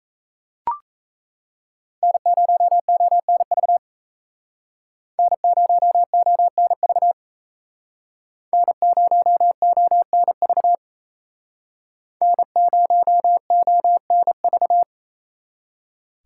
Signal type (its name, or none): Morse code